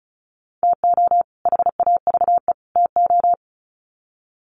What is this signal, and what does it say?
Morse code, heard as a keyed tone, about 34 wpm, 700 Hz. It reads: TO HAVE TO
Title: Sets of 3 Words 35wpm